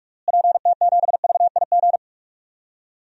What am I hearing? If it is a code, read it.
Morse code, heard as a keyed tone, about 46 wpm, 700 Hz. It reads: PT7VIG
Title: DXCC-100 Call Signs - 6 Characters - Repeated 3x - 45wpm